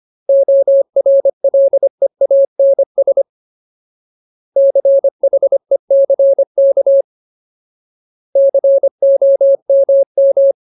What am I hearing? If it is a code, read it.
Morse code, heard as a keyed tone, about 25 wpm, 550 Hz. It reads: ORLEANS CHECK COMM
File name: Ultimate Word List / 25WPM Ultimate